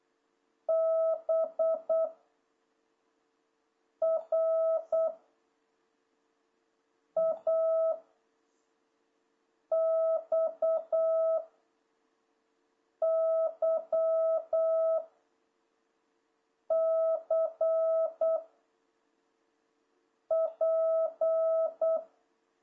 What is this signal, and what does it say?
Morse code, heard as a keyed tone, about 8 wpm, 650 Hz. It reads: B R A X Y C P
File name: jh8cbh / 2026cw